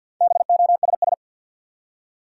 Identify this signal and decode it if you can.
Morse code, heard as a keyed tone, about 53 wpm, 700 Hz. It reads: BOSS
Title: Four Letter Words - 50wpm